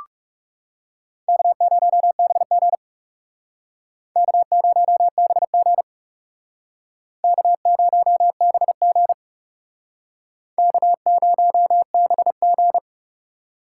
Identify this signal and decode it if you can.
Morse code, heard as a keyed tone, about 39 wpm, 700 Hz. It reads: K0BG K0BG K0BG K0BG